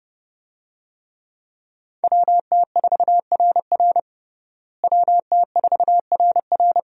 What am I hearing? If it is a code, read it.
Morse code, heard as a keyed tone, about 30 wpm, 700 Hz. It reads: WT4RR WT4RR
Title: USA Call Signs - 5 Characters - Repeated 2x - 30wpm